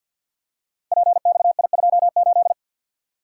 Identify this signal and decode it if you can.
Morse code, heard as a keyed tone, about 53 wpm, 700 Hz. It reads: PXI28
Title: Code Groups - 5 Letter:Numbers - 50wpm